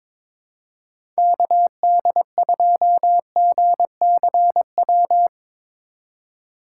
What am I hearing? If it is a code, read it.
Morse code, heard as a keyed tone, about 22 wpm, 700 Hz. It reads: KD2GCW
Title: USA Call Signs - 2x Character Spacing - 22wpm